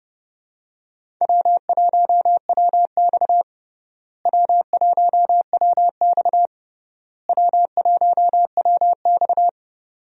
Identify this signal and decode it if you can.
Morse code, heard as a keyed tone, about 32 wpm, 700 Hz. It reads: W1WX W1WX W1WX
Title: USA Call Signs - 4 Characters - Repeated 3x - 30wpm